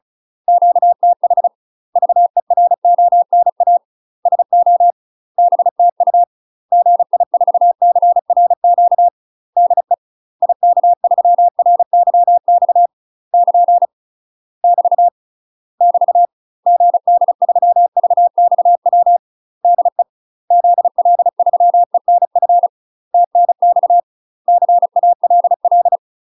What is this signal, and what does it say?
Morse code, heard as a keyed tone, about 36 wpm, 700 Hz. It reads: QTH VERONA SO BTU GI4CRQ DE IK3RYX <KN> <BT> <BT> GD3VXW DE ZL3ENF TNX CALL
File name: UK MASTER QSOs TRAINING / 35WPM / UK MASTER QSOs 0018